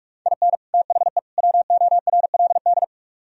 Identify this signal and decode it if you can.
Morse code, heard as a keyed tone, about 44 wpm, 700 Hz. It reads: IN THE WORLD